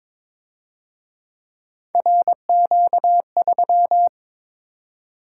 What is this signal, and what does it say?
Morse code, heard as a keyed tone, about 26 wpm, 700 Hz. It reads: RQ3
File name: Speed-Racing - Three Letter-Number Combinations - 22wpm